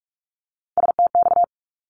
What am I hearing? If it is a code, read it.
Morse code, heard as a keyed tone, about 46 wpm, 700 Hz. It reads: STX
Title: ARRL Sections 45wpm